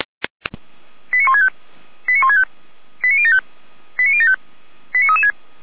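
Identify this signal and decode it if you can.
Five-tone selective call (ZVEI), heard as quick runs of five tones, tones of about 70 ms. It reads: 89165 89165 89075 89075 89336
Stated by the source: ZVEI five-tone selective call received by SDR.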